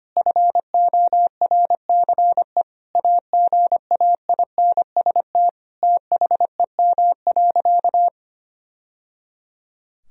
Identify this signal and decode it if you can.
Morse code, heard as a keyed tone, about 25 wpm, 700 Hz. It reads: FORCE AGAINST THEM.
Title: Sentences from Top 1000 Words - Review Part 4 - 25wpm